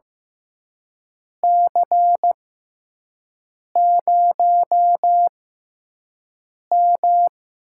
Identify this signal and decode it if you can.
Morse code, heard as a keyed tone, about 15 wpm, spaced as for 5 wpm, 700 Hz. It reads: C0M
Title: USA Call Signs - 4 Characters - 6x Character Spacing - 15wpm